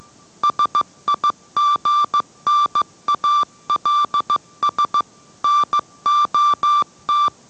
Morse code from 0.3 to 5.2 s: SIGNALS